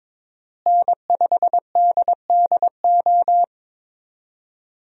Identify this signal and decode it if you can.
Morse code, heard as a keyed tone, about 22 wpm, 700 Hz. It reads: N5DDO
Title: USA Call Signs - 5 Characters - Repeated 3x - 22wpm